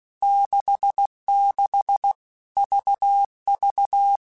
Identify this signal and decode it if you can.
Morse code, heard as a keyed tone, about 16 wpm, 800 Hz. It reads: 66 VV
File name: Basic Letter Groups By7Groups / VB46 / VB46@15